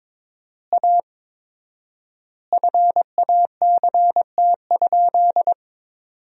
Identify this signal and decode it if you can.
Morse code, heard as a keyed tone, about 22 wpm, 700 Hz. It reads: A FACT?
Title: 4x Word-Spacing - Sentences from Top 500 Words - 22wpm